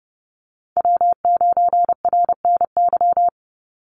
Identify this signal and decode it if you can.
Morse code, heard as a keyed tone, about 30 wpm, 700 Hz. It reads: W9RNY
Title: Call Signs - 30wpm